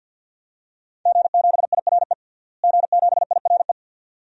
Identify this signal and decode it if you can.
Morse code, heard as a keyed tone, about 52 wpm, 700 Hz. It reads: G7IRE G7IRE
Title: DXCC-100 Call Signs - 5 Characters - Repeated 2x - 50wpm